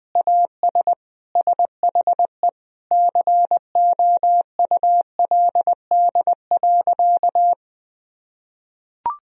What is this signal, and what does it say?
Morse code, heard as a keyed tone, about 20 wpm, 700 Hz. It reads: AS SHE COULD.
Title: Sentences from Top 1000 Words - Review Part 3 - 20wpm